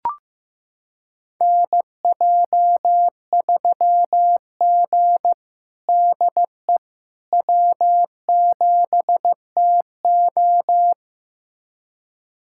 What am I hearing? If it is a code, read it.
Morse code, heard as a keyed tone, about 15 wpm, 700 Hz. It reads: NJ3G DE W7TO
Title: USA Callsign DE Callsign - 4 Characters - 15wpm